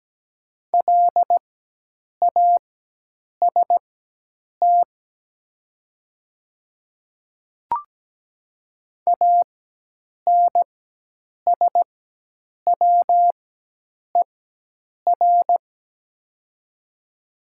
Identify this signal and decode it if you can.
Morse code, heard as a keyed tone, about 17 wpm, spaced as for 8 wpm, 700 Hz. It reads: LAST ANSWER
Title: 4x Character Spacing - Top 300 Words - 17wpm